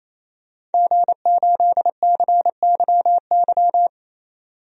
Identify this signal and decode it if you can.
Morse code, heard as a keyed tone, about 28 wpm, 700 Hz. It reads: G8CYY
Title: DXCC-100 Call Signs - 5 Characters - 28wpm